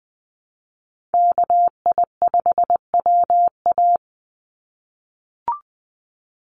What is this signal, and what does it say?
Morse code, heard as a keyed tone, about 20 wpm, 690 Hz. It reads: KI5WA
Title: NA SOTA Chaser Callsigns 20wpm